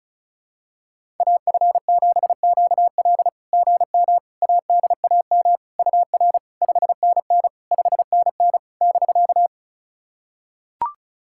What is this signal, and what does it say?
Morse code, heard as a keyed tone, about 36 wpm, 690 Hz. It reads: AF7QL GM ADAM UR 5NN 5NN <BK>
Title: SOTA QSO Practice 35wpm